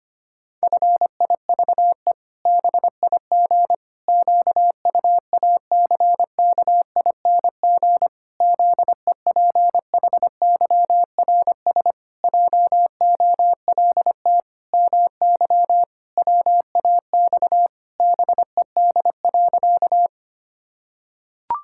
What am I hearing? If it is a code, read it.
Morse code, heard as a keyed tone, about 25 wpm, 700 Hz. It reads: FIVE BIG QUACKING ZEPHYRS JOLT MY WAX BED.
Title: Pangram Challenge v2 - 25wpm